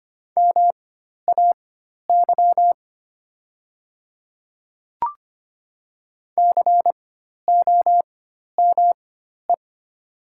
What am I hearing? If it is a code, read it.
Morse code, heard as a keyed tone, about 25 wpm, spaced as for 12 wpm, 700 Hz. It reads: MAY COME
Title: 4x Character Spacing - Top 100 Words - 25wpm